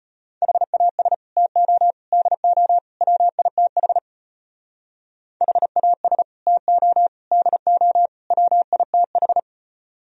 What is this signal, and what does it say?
Morse code, heard as a keyed tone, about 30 wpm, 700 Hz. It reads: HAS TO DO WITH HAS TO DO WITH
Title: Speed-Racing - Sets of 4 Words 25wpm